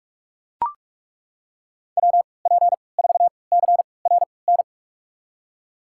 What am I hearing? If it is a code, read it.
Morse code, heard as a keyed tone, about 42 wpm, spaced as for 25 wpm, 700 Hz. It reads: WP4CRN
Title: USA Call Signs - 6 Characters - 3x Character Spacing - 45wpm